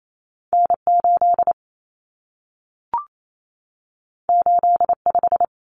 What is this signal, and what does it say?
Morse code, heard as a keyed tone, about 28 wpm, 700 Hz. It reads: N8 85
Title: Common Cut Numbers - 2-Digit 28wpm